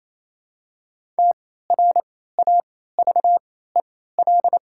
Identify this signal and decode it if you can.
Morse code, heard as a keyed tone, about 28 wpm, spaced as for 16 wpm, 700 Hz. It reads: TRAVEL
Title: 3x Character Spacing - Top 500 Words - 28wpm